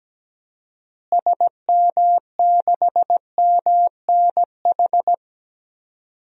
Morse code, heard as a keyed tone, about 17 wpm, 700 Hz. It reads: SM6MNH
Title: DXCC-100 Call Signs - 6 Characters - Repeated 2x - 17wpm